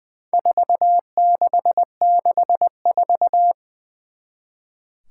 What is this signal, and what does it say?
Morse code, heard as a keyed tone, about 21 wpm, 700 Hz. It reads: 4664
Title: Four Challenging Letter-Number Combinations 20wpm